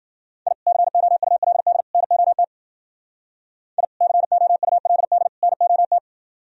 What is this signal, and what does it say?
Morse code, heard as a keyed tone, about 46 wpm, 700 Hz. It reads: I COULD NOT I COULD NOT
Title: Speed-Racing - Sets of 3 Words 40wpm